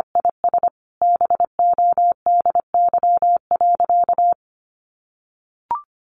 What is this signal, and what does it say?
Morse code, heard as a keyed tone, about 25 wpm, 700 Hz. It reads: IS BODY.
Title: A Sound Of Thunder 25wpm